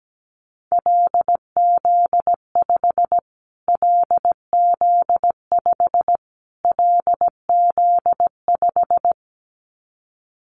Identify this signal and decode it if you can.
Morse code, heard as a keyed tone, about 17 wpm, 700 Hz. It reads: LZ5 LZ5 LZ5
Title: Callsign Prefixes for Top 25 DXCC - Repeated 3x - 17wpm